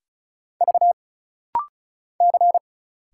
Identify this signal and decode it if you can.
Morse code, heard as a keyed tone, about 33 wpm, 700 Hz. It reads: V C
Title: Single Letter-Number Alt - Rapid-Fire 35wpm